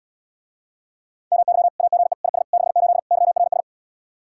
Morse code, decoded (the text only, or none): MY AGE IS 68 YRS